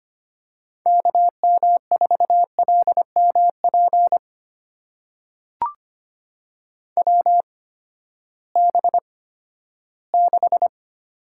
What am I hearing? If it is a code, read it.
Morse code, heard as a keyed tone, about 25 wpm, spaced as for 14 wpm, 700 Hz. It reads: KM4LMP WB6
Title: USA Call Signs - 6 Characters - 8x Character Spacing - 25wpm